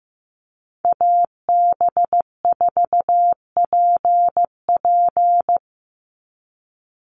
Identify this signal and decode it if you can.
Morse code, heard as a keyed tone, about 15 wpm, 700 Hz. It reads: AB4PP
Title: NA SOTA Chaser Callsigns 15wpm